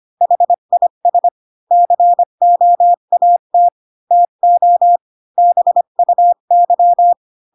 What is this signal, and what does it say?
Morse code, heard as a keyed tone, about 25 wpm, 700 Hz. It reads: HIS COAT TO BUY
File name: Pinocchio / Pinocchio ch8 20 25 ik1hge